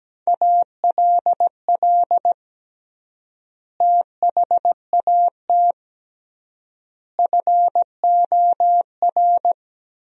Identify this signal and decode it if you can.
Morse code, heard as a keyed tone, about 17 wpm, 700 Hz. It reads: ALL THAT FOR
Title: 3x Word-Spacing - Sentences from Top 500 Words - 17wpm